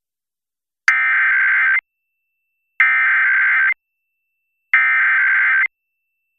Emergency Alert System (SAME) tones (header bursts), heard as harsh buzzing bursts. It is heard from 0.9 to 5.6 s.